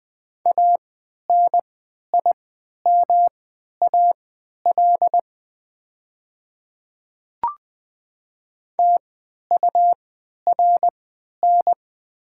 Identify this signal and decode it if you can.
Morse code, heard as a keyed tone, about 20 wpm, spaced as for 11 wpm, 700 Hz. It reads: ANIMAL TURN
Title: 3x Character Spacing - Top 500 Words - 20wpm